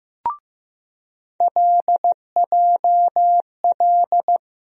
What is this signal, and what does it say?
Morse code, heard as a keyed tone, about 15 wpm, 700 Hz. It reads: LJL
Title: USA Call Sign Postfixes - 3 Letters 15wpm